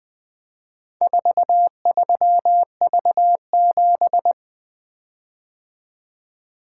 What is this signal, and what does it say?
Morse code, heard as a keyed tone, about 22 wpm, 700 Hz. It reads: 43V7
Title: Speed-Racing - Four Letter-Number Combinations - 20wpm